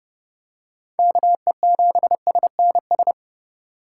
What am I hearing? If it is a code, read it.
Morse code, heard as a keyed tone, about 30 wpm, 700 Hz. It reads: KE7SNS
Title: USA Call Signs - 6 Characters - 4x Character Spacing - 30wpm